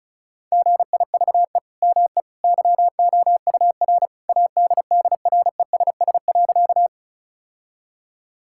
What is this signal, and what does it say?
Morse code, heard as a keyed tone, about 36 wpm, 700 Hz. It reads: GIVE ME YOUR ADDRESS.
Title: Sentences from Top 1000 Words - Review Part 6 - 35wpm